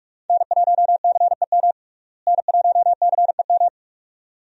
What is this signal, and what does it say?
Morse code, heard as a keyed tone, about 46 wpm, 700 Hz. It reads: N1CEM N1CEM
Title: USA Call Signs - 5 Characters - Repeated 2x - 45wpm